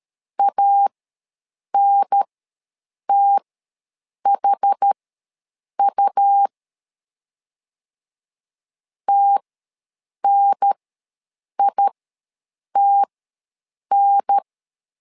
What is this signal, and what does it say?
Morse code, heard as a keyed tone, about 13 wpm, spaced as for 7 wpm, 800 Hz. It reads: ANTHU TNITN
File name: LEZIONI / 2 UHDN / 2 UHDN ws 12wpm 1708